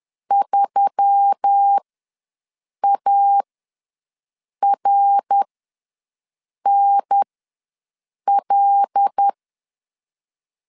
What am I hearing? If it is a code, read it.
Morse code, heard as a keyed tone, about 10 wpm, spaced as for 6 wpm, 800 Hz. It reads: 3ARNL
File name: LEZIONI / 5 LBVC / 5 L ws 10wpm 3636